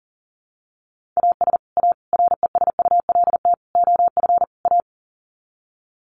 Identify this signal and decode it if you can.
Morse code, heard as a keyed tone, about 40 wpm, 700 Hz. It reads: AS A RESULT OF A
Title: Sets of 5 Words Encore 40wpm